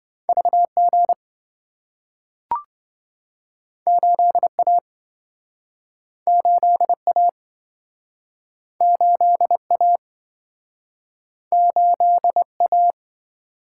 Morse code, heard as a keyed tone, about 27 wpm, 700 Hz. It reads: VG 8A 8A 8A 8A